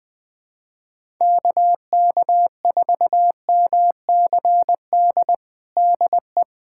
Morse code, heard as a keyed tone, about 21 wpm, 700 Hz. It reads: KK4MCD DE
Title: USA Callsign DE Callsign - 6 Characters - 20wpm